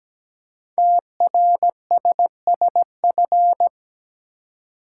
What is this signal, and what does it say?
Morse code, heard as a keyed tone, about 17 wpm, 700 Hz. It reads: TRSSF
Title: Code Groups - 5 Letters - 17wpm